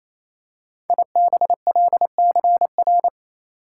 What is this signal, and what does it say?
Morse code, heard as a keyed tone, about 27 wpm, 700 Hz. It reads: IBLCR
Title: Code Groups - 5 Letter:Numbers - 28wpm